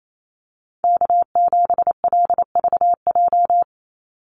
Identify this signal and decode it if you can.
Morse code, heard as a keyed tone, about 28 wpm, 700 Hz. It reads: K7LVJ
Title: Call Signs Repeated 2x - 28wpm